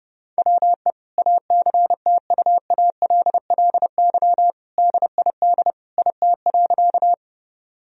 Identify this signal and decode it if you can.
Morse code, heard as a keyed tone, about 30 wpm, 700 Hz. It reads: WE ACTUALLY DID IT.
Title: Sentences from Top 1000 Words - Review Part 4 - 30wpm